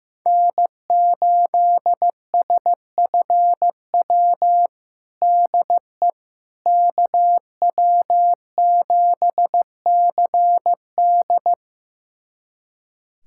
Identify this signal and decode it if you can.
Morse code, heard as a keyed tone, about 15 wpm, 700 Hz. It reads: N8SFW DE KW7CD
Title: USA Callsign DE Callsign - 5 Characters - 15wpm